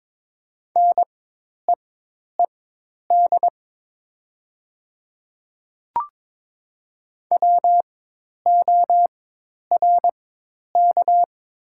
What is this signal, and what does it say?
Morse code, heard as a keyed tone, about 22 wpm, spaced as for 10 wpm, 700 Hz. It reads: NEED WORK
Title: 4x Character Spacing - Top 300 Words - 22wpm